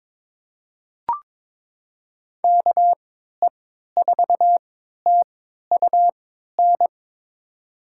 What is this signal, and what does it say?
Morse code, heard as a keyed tone, about 22 wpm, spaced as for 13 wpm, 700 Hz. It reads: KE4TUN
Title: USA Call Signs - 6 Characters - 3x Character Spacing - 22wpm